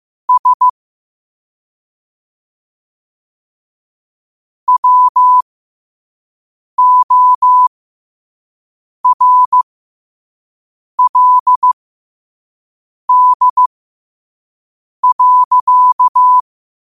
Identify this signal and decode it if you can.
Morse code, heard as a keyed tone, about 15 wpm, spaced as for 5 wpm, 1000 Hz. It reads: S WORLD.
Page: Quotes for Fri, 15 Aug 2025 in Morse Code at 5 words per minute.